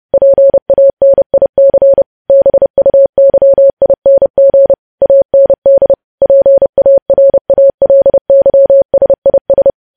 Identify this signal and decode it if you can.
Morse code, heard as a keyed tone, about 30 wpm, 550 Hz. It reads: PANIC BUYING AND PARALYSIS
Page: news headlines in morse code at 30 wpm